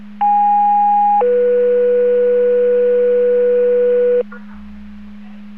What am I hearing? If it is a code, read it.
Two-tone page, heard as a one-second tone, then a three-second tone, about 820 then 490 Hz, with noise. It is heard from 0.2 to 4.2 s.